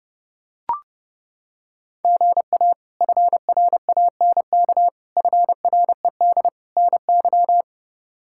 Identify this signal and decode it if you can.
Morse code, heard as a keyed tone, about 31 wpm, 700 Hz. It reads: GA FRANK FRED NY
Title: K1USN SST QSO Practice - 30wpm